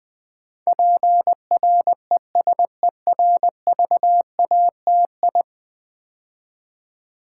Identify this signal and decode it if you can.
Morse code, heard as a keyed tone, about 20 wpm, 700 Hz. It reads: PRESERVATI
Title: Build A Word - 5 Letters At A Time - 12-Letter Words - 20wpm